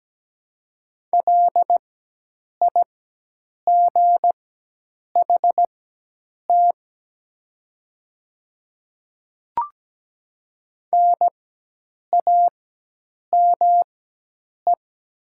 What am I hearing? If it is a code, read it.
Morse code, heard as a keyed tone, about 17 wpm, spaced as for 8 wpm, 700 Hz. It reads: LIGHT NAME
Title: 4x Character Spacing - Top 200 Words - 17wpm